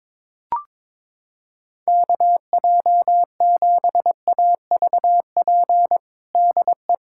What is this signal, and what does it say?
Morse code, heard as a keyed tone, about 22 wpm, 700 Hz. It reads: KJ7AVP DE
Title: USA Callsign DE Callsign 22wpm